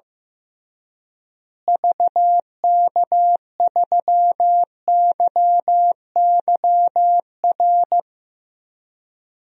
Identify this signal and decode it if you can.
Morse code, heard as a keyed tone, about 15 wpm, 700 Hz. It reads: VK3YYR
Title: DXCC-100 Call Signs - 6 Characters - Repeated 2x - 15wpm